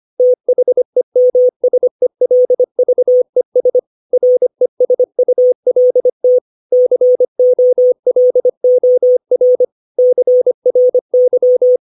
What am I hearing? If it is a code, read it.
Morse code, heard as a keyed tone, about 25 wpm, 500 Hz. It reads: THEMSELVES RESULT COLOR CRY
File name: 632 words / 632 Words 25WPM